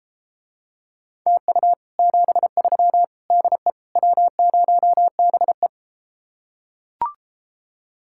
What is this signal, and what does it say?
Morse code, heard as a keyed tone, about 33 wpm, 700 Hz. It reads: TU 73 DE W0BE